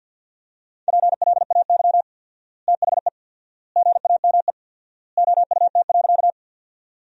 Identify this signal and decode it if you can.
Morse code, heard as a keyed tone, about 50 wpm, 700 Hz. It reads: PLAY THE GAME OUT.
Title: 4x Word-Spacing - Sentences from Top 500 Words - 50wpm